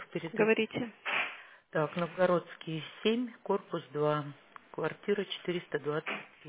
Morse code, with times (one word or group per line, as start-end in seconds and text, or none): none